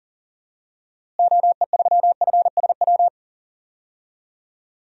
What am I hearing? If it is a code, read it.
Morse code, heard as a keyed tone, about 40 wpm, 700 Hz. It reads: OE3FSW
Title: DXCC-100 Call Signs - 6 Characters - Repeated 3x - 40wpm